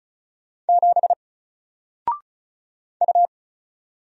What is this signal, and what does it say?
Morse code, heard as a keyed tone, about 36 wpm, 700 Hz. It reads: 7 U
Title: Single Character (Letter:Number:Punctuation:Prosign) Alt 35wpm